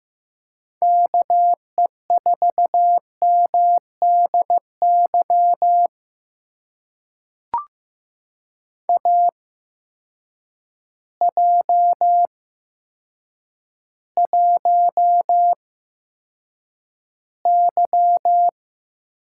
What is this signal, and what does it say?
Morse code, heard as a keyed tone, about 15 wpm, spaced as for 8 wpm, 700 Hz. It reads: KE4MDY AJ1Y
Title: USA Call Signs - 8x Character Spacing - 15wpm